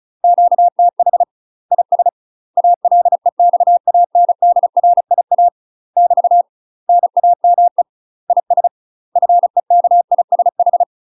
Morse code, heard as a keyed tone, about 35 wpm, 700 Hz. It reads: QTH IS ALEXANDRIA <BT> NAME IS FEKISH
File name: UK MASTER QSOs TRAINING / 35WPM / UK MASTER QSOs 0029